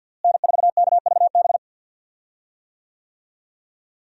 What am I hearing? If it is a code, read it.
Morse code, heard as a keyed tone, about 52 wpm, 700 Hz. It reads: N4KVB
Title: USA Call Signs - 5 Characters - Repeated 3x - 50wpm